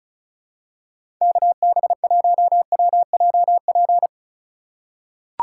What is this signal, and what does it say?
Morse code, heard as a keyed tone, about 36 wpm, 700 Hz. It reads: KB1WJP
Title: USA Call Signs - 2x Character Spacing - 35wpm